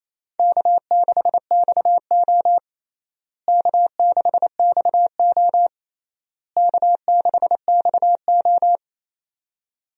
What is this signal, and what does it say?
Morse code, heard as a keyed tone, about 28 wpm, 700 Hz. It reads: K6XO K6XO K6XO
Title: USA Call Signs - 4 Characters - Repeated 3x - 28wpm